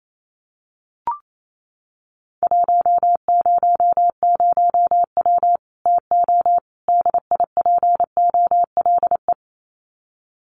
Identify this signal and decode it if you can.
Morse code, heard as a keyed tone, about 28 wpm, 700 Hz. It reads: 100W TO DIPOLE
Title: Common QSO Phrases 28wpm